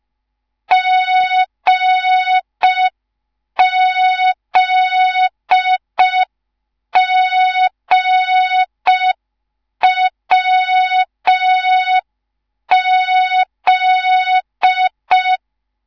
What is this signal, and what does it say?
Morse code, heard as a keyed tone, about 5 wpm, 3700 Hz. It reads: GZGWZ
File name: Morze-HA2MN / hang / 10GWZ HA2MN